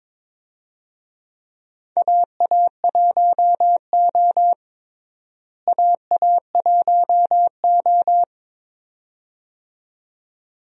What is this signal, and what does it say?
Morse code, heard as a keyed tone, about 22 wpm, 700 Hz. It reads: AA1O AA1O
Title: USA Call Signs - 4 Characters - Repeated 2x - 22wpm